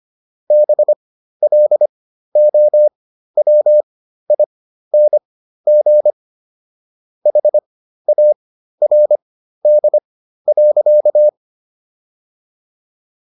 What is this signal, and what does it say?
Morse code, heard as a keyed tone, about 25 wpm, spaced as for 13 wpm, 600 Hz. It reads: BLOWING HARD.